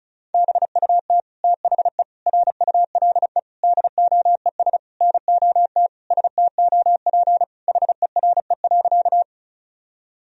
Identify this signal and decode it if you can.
Morse code, heard as a keyed tone, about 36 wpm, 700 Hz. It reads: BUT THE RULE DOES NOT STOP HERE.
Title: Sentences from Top 1000 Words - Review Part 2 - 35wpm